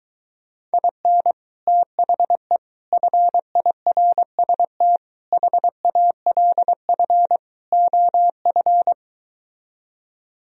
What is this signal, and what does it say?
Morse code, heard as a keyed tone, about 23 wpm, 700 Hz. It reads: IN THE FIRST HALF OF